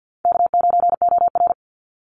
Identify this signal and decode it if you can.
Morse code, heard as a keyed tone, about 50 wpm, 700 Hz. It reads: K9OR
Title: Easy Call Signs - 50wpm